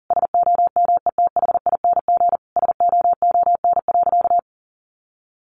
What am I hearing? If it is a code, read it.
Morse code, heard as a keyed tone, about 37 wpm, 700 Hz. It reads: SOMETHING SOON.
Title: Sentences from Top 1500 Words 40wpm